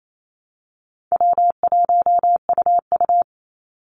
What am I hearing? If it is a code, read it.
Morse code, heard as a keyed tone, about 28 wpm, 700 Hz. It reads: W1UU
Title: Easy Call Signs Repeated 3x - 28wpm